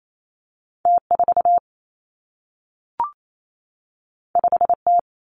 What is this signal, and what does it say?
Morse code, heard as a keyed tone, about 28 wpm, 700 Hz. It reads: T 4 5 T
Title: Two Challenging Letter-Number Combinations 28wpm